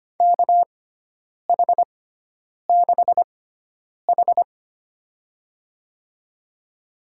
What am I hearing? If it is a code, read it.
Morse code, heard as a keyed tone, about 25 wpm, spaced as for 9 wpm, 700 Hz. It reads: KH6H
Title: USA Call Signs - 4 Characters - 6x Character Spacing - 25wpm